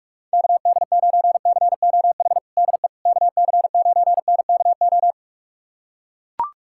Morse code, heard as a keyed tone, about 41 wpm, 700 Hz. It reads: KD9COH DE KC9NKO
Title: USA Callsign DE Callsign - 6 Characters - 45wpm